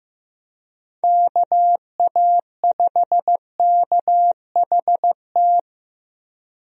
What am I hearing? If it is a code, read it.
Morse code, heard as a keyed tone, about 15 wpm, 700 Hz. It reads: KA5KHT
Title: USA Call Signs - 6 Characters - Repeated 3x - 15wpm